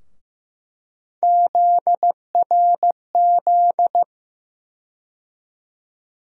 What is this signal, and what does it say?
Morse code, heard as a keyed tone, about 16 wpm, 700 Hz. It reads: ZRZ
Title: USA Call Sign Postfixes - 3 Letters 15wpm